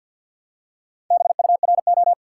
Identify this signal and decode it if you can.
Morse code, heard as a keyed tone, about 50 wpm, 700 Hz. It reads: BURO
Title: Top 180 QSO Elements 50wpm